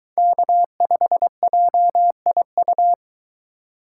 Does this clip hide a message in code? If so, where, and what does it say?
Yes: Morse code from 0.2 to 2.9 s, heard as a keyed tone, about 24 wpm, 700 Hz. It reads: K5JIU